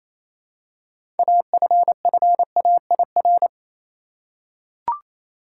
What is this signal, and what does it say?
Morse code, heard as a keyed tone, about 27 wpm, 700 Hz. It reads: AFFAIR
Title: Six Letter Words - 28wpm